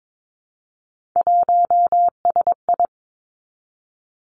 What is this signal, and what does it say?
Morse code, heard as a keyed tone, about 22 wpm, 700 Hz. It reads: 1SI
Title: Three Challenging Letter-Number Combinations 22wpm